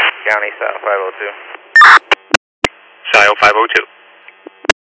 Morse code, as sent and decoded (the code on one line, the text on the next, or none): none